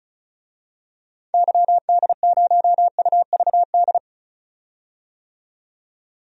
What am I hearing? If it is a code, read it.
Morse code, heard as a keyed tone, about 34 wpm, 700 Hz. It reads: YD0UVD
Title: DXCC-100 Call Signs - 6 Characters - 35wpm